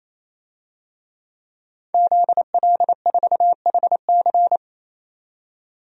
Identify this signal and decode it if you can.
Morse code, heard as a keyed tone, about 28 wpm, 700 Hz. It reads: ZL4HC
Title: DXCC-100 Call Signs - 5 Characters - 28wpm